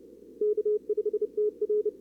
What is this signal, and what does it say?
Morse code, heard as a keyed tone, about 36 wpm, 410 Hz. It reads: K5TR
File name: CWSprint / Feb2022 / k5tr-feb-na-sprint-cw-k5ot -0000